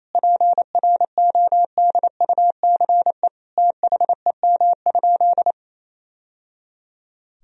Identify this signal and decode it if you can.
Morse code, heard as a keyed tone, about 28 wpm, 700 Hz. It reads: PRODUCE THEM?
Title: Sentences from Top 1000 Words - Review Part 3 - 28wpm